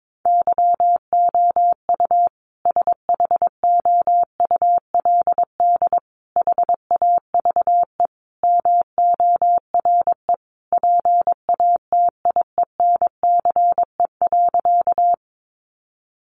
Morse code, heard as a keyed tone, about 22 wpm, 700 Hz. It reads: YOU SHOULD HAVE MORE PATIENCE.
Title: Sentences from Top 1300 Words 22wpm